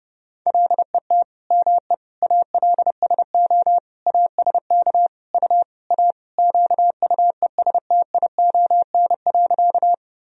Morse code, heard as a keyed tone, about 30 wpm, 700 Hz. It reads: LET ME ALSO ASK U A QUESTION.
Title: Sentences from Top 400 Words and Abbreviations 30wpm